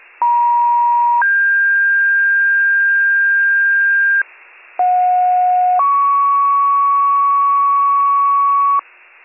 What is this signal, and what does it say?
Two-tone page, heard as a one-second tone, then a three-second tone, about 950 then 1600 Hz, about 730 then 1120 Hz, with noise. It is heard from 0.2 to 8.8 s.